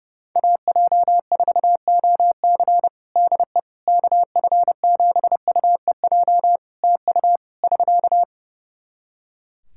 Morse code, heard as a keyed tone, about 32 wpm, 700 Hz. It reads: AJ4OC DE KF7UEJ TU <SK>
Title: Practice QSO - Exchange 4 - Ending - 30wpm